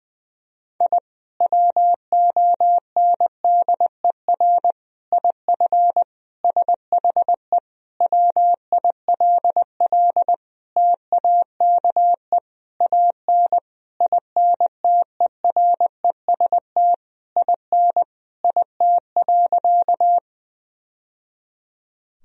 Morse code, heard as a keyed tone, about 21 wpm, 700 Hz. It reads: I WONDER IF SHE WILL TAKE AN INTEREST IN IT.
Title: Sentences from Top 1000 Words - Review Part 3 - 20wpm